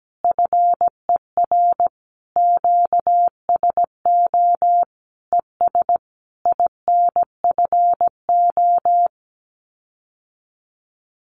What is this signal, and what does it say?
Morse code, heard as a keyed tone, about 17 wpm, 700 Hz. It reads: FER QSO ES INFO
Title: Common QSO Phrases Encore 17wpm